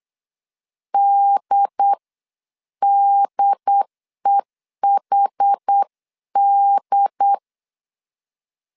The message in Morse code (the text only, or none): DDEHD